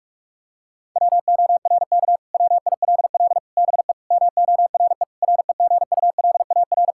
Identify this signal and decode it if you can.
Morse code, heard as a keyed tone, about 41 wpm, 700 Hz. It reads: WORK WILL BE MORE REGULAR
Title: Sentences from Top 1000 Words - Review Part 6 - 45wpm